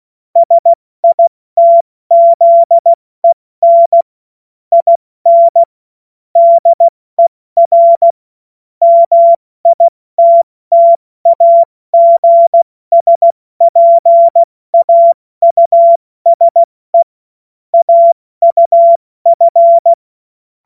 Morse code, heard as a keyed tone, about 16 wpm, 680 Hz. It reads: SITZEN IN DER MITTAGSPAUSE AUF
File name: Morsekurs / Witze / Witz 01. November 2022 70 F80